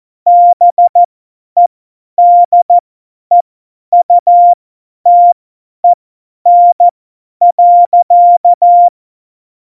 Morse code, heard as a keyed tone, about 14 wpm, spaced as for 10 wpm, 700 Hz. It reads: BEDEUTEN.
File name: Morsekurs / Witze / Witz 05 Mai 2020, 50 F70